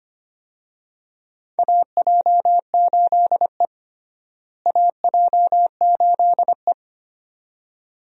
Morse code, heard as a keyed tone, about 25 wpm, 700 Hz. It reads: AJ8E AJ8E
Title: USA Call Signs - 4 Characters - Repeated 2x - 25wpm